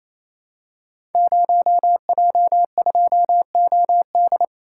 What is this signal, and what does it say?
Morse code, heard as a keyed tone, about 27 wpm, 700 Hz. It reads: 0J2OD
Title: Code Groups - 5 Letter:Numbers - 28wpm